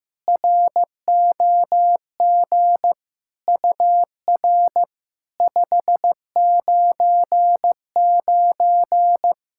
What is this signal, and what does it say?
Morse code, heard as a keyed tone, about 15 wpm, 700 Hz. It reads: ROG UR 599
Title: SOTA QSO Practice 15wpm